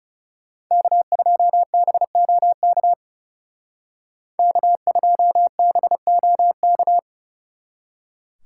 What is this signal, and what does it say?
Morse code, heard as a keyed tone, about 37 wpm, 700 Hz. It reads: K2BOK K2BOK
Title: Speed Racing - USA 1x3 Call Signs 30wpm